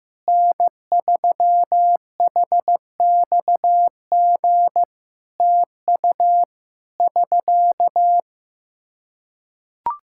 Morse code, heard as a keyed tone, about 15 wpm, 700 Hz. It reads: N3HXG TU <SK>
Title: Practice QSO - Exchange 4 - Ending - 15wpm